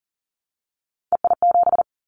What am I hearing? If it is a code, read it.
Morse code, heard as a keyed tone, about 38 wpm, 700 Hz. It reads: EI7
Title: Callsign Prefixes for Top 50 DXCC - 40wpm